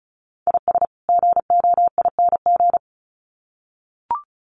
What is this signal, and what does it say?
Morse code, heard as a keyed tone, about 34 wpm, 700 Hz. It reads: IS GOING
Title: Sets of 2 Words 35wpm